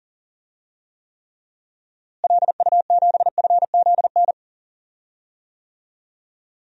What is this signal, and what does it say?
Morse code, heard as a keyed tone, about 40 wpm, 700 Hz. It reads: LU7FZN
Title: DXCC-100 Call Signs - 6 Characters - Repeated 2x - 40wpm